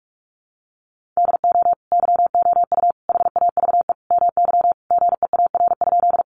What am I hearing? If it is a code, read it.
Morse code, heard as a keyed tone, about 44 wpm, 700 Hz. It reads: DO YOU HAVE MY GEAR?
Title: Sentences from Top 1400 Words 45wpm